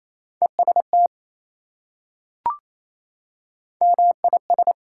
Morse code, heard as a keyed tone, about 28 wpm, 700 Hz. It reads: EST MIS
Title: Prefixes and Suffixes 28wpm